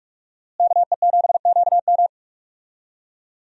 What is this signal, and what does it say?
Morse code, heard as a keyed tone, about 44 wpm, 700 Hz. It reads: KE7QM
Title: USA Call Signs - 5 Characters - Repeated 2x - 45wpm